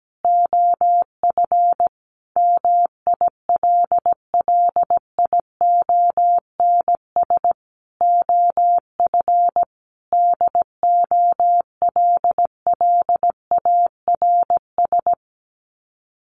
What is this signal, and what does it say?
Morse code, heard as a keyed tone, about 17 wpm, 700 Hz. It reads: OF MILLIONS OF DOLLARS
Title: Sets of 5 Words 17wpm